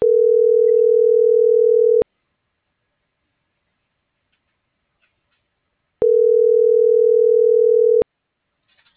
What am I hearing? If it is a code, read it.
Ringback tone (North America), 440 + 480 Hz.